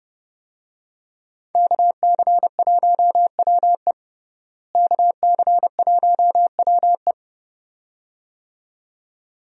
Morse code, heard as a keyed tone, about 30 wpm, 700 Hz. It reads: KC1WE KC1WE
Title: USA Call Signs - 5 Characters - Repeated 2x - 30wpm